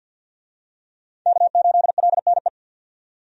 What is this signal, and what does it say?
Morse code, heard as a keyed tone, about 46 wpm, 700 Hz. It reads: K8LNE
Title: USA Call Signs - 5 Characters - 50wpm